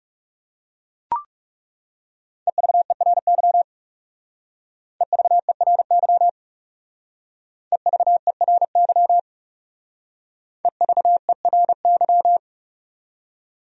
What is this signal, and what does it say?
Morse code, heard as a keyed tone, about 38 wpm, 700 Hz. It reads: EVERY EVERY EVERY EVERY